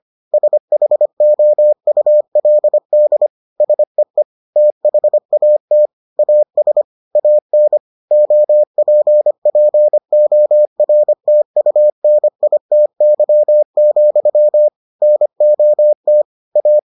Morse code, heard as a keyed tone, about 25 wpm, 600 Hz. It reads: SHOULD SEE THAT AS AN OPPORTUNITY, NOT A